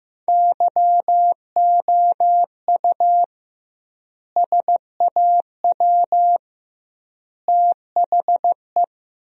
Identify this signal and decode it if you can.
Morse code, heard as a keyed tone, about 15 wpm, 700 Hz. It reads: YOU SAW THE
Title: 2x Word-Spacing - Sentences from Top 400 Words - 15wpm